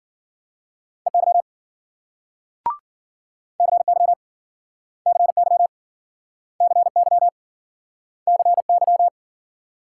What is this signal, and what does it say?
Morse code, heard as a keyed tone, about 57 wpm, 700 Hz. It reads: EY CY CY CY CY